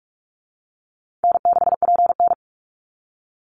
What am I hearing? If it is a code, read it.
Morse code, heard as a keyed tone, about 44 wpm, 700 Hz. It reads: N6PN
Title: Easy Call Signs Repeated 2x - 45wpm